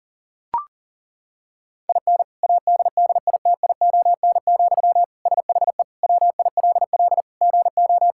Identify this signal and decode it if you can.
Morse code, heard as a keyed tone, about 40 wpm, 700 Hz. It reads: IN ADDITION, SHE WILL GO
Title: Sentences from Top 1000 Words - Review Part 6 - 40wpm